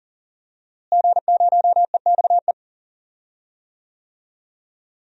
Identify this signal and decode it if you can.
Morse code, heard as a keyed tone, about 37 wpm, 700 Hz. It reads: G0EXE
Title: DXCC-100 Call Signs - 5 Characters - Repeated 2x - 40wpm